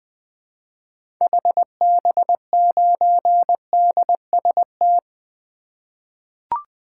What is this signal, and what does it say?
Morse code, heard as a keyed tone, about 21 wpm, 700 Hz. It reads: HB9DST
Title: 2023 SOTA Activator Call Signs - 20wpm